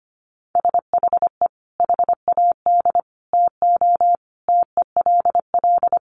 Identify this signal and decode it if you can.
Morse code, heard as a keyed tone, about 25 wpm, 700 Hz. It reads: SHE HAD TO TELL
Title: Sentences from Top 200 Words 25wpm